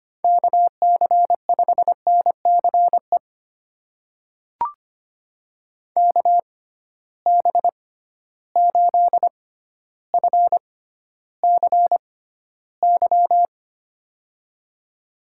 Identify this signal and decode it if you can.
Morse code, heard as a keyed tone, about 25 wpm, spaced as for 13 wpm, 700 Hz. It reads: KC5NCE KB8FCY
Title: USA Call Signs - 6 Characters - 6x Character Spacing - 25wpm